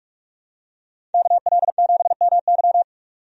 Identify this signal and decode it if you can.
Morse code, heard as a keyed tone, about 42 wpm, 700 Hz. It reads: KL7MY
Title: USA Call Signs - 5 Characters - 45wpm